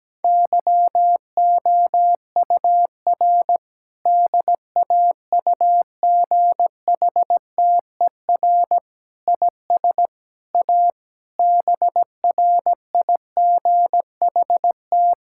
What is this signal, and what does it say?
Morse code, heard as a keyed tone, about 17 wpm, 700 Hz. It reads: YOUR DAUGHTER IS A BRIGHT
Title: Sentences from Top 1600 Words 17wpm